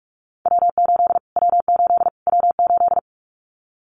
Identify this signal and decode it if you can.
Morse code, heard as a keyed tone, about 44 wpm, 700 Hz. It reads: W8 W8 W8
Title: Callsign Prefixes for Top 25 DXCC - Repeated 3x - 45wpm